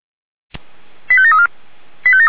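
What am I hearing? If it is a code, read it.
Five-tone selective call (ZVEI), heard as quick runs of five tones, tones of about 70 ms. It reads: 85734
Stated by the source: ZVEI five-tone fire brigade call 89494 received on an SDR, logged by Feuerwehr Orlamünde.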